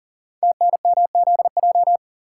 Morse code, heard as a keyed tone, about 41 wpm, 700 Hz. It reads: TNMZJ
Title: Code Groups - 5 Letters - 40wpm